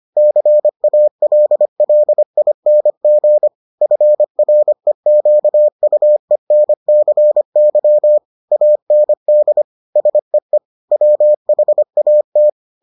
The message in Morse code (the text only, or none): CALLING FREQUENCY AND SEE WHAT